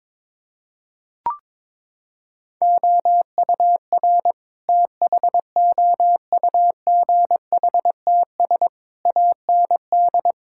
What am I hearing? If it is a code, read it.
Morse code, heard as a keyed tone, about 21 wpm, 700 Hz. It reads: OUR THOUGHTS AND
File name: practice / 22wpm / Sets of 8 Words - 22wpm